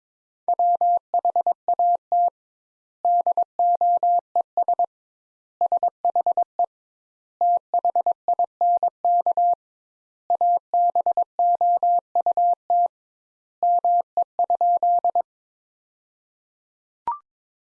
Morse code, heard as a keyed tone, about 22 wpm, 700 Hz. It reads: WHAT DOES SHE THINK ABOUT ME?
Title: Sentences Top 200 Words - 22wpm